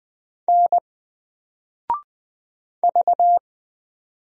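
Morse code, heard as a keyed tone, about 20 wpm, 700 Hz. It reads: N V
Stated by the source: Single letter and number Morse drill at 20 wpm, from N4LCD.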